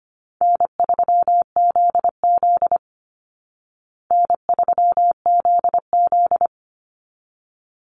Morse code, heard as a keyed tone, about 25 wpm, 700 Hz. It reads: N3ZZ N3ZZ
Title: Call Signs Repeated 3x - 25wpm